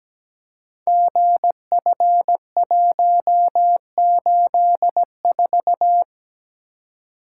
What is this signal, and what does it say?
Morse code, heard as a keyed tone, about 17 wpm, 700 Hz. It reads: GF184
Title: Code Groups - 5 Letter:Numbers - 17wpm